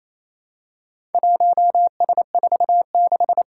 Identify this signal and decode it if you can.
Morse code, heard as a keyed tone, about 28 wpm, 700 Hz. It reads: 1S46
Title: Four Challenging Letter-Number Combinations 28wpm